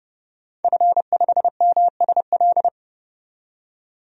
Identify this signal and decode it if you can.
Morse code, heard as a keyed tone, about 30 wpm, 700 Hz. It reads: F5MSL
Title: DXCC-100 Call Signs - 5 Characters - 30wpm